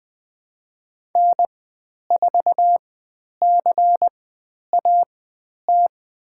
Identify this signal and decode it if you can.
Morse code, heard as a keyed tone, about 20 wpm, spaced as for 10 wpm, 700 Hz. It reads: N4CAT
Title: 109 Lesson 18g - Letter m - Call Signs 20wpm